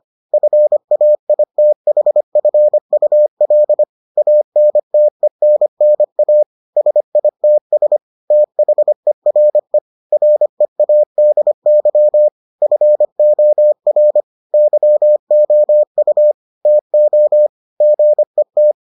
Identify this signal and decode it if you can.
Morse code, heard as a keyed tone, about 25 wpm, 600 Hz. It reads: FAITHFUL ANTENNA SITS THERE READY FOR YOU TO GET